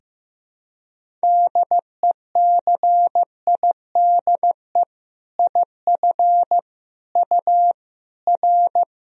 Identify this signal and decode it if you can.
Morse code, heard as a keyed tone, about 15 wpm, 700 Hz. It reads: DECIDE IF U R
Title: Sentences from Top 500 Words and Abbreviations 15wpm